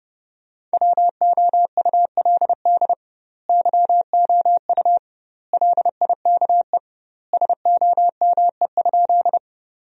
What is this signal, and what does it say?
Morse code, heard as a keyed tone, about 30 wpm, 700 Hz. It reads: WOULD YOU LIKE SOME?
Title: 2x Word-Spacing - Sentences from Top 100 Words - 30wpm